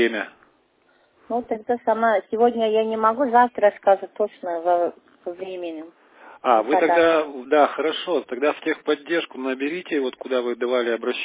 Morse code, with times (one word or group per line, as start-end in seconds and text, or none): none